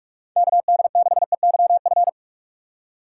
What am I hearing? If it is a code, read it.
Morse code, heard as a keyed tone, about 46 wpm, 700 Hz. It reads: KD6EYF
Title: USA Call Signs - 6x Character Spacing - 45wpm